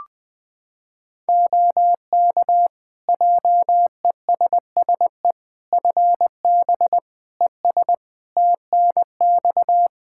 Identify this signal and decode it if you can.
Morse code, heard as a keyed tone, about 20 wpm, 700 Hz. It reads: OK JESSE FB ES TNX
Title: Practice QSO - Exchange 2 - Info - 20wpm